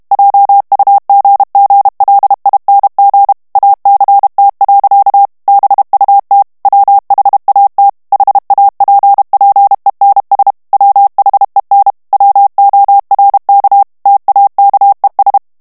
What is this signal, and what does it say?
Morse code, heard as a keyed tone, about 32 wpm, 800 Hz. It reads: JUGGLING ACT. BUT WHAT HAPPENS WHEN WORK TAKES
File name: morse mp3 / ANGLAIS texte clair 1 vitesse 30 mots minute